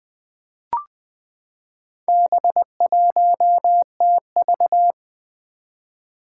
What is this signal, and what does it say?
Morse code, heard as a keyed tone, about 20 wpm, 700 Hz. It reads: B1TV
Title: Four Challenging Letter-Number Combinations 20wpm